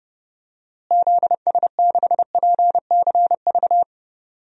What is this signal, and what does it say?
Morse code, heard as a keyed tone, about 30 wpm, 700 Hz. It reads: ZS6PCV
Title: DXCC-100 Call Signs - 6 Characters - Repeated 2x - 30wpm